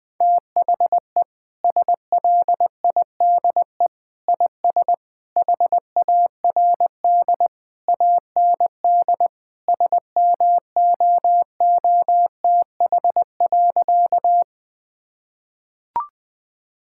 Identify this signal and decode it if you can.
Morse code, heard as a keyed tone, about 20 wpm, 700 Hz. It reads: THE SLIDE IS HARD AND SMOOTH.
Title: Sentences from Top 1600 Words 20wpm